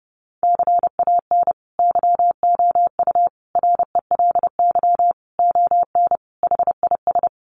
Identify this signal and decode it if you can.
Morse code, heard as a keyed tone, about 30 wpm, 700 Hz. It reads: CAN YOU RELY ON HIS
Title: Sentences from Top 1200 Words 30wpm